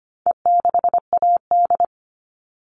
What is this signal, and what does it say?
Morse code, heard as a keyed tone, about 25 wpm, 700 Hz. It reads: E6AD
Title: All Cut Numbers - 4-Digit 25wpm